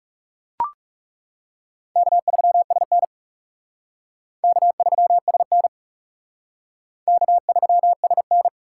Morse code, heard as a keyed tone, about 36 wpm, 700 Hz. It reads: K3SN K3SN K3SN
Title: Speed Racing - USA Call Signs 30wpm